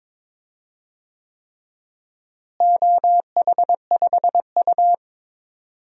Morse code, heard as a keyed tone, about 22 wpm, 700 Hz. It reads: OH5U
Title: DXCC-100 Call Signs - 4 Characters - Repeated 3x - 22wpm